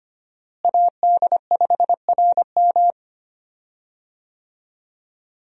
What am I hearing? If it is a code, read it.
Morse code, heard as a keyed tone, about 25 wpm, 700 Hz. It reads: AD5RM
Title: USA Call Signs - 5 Characters - Repeated 2x - 25wpm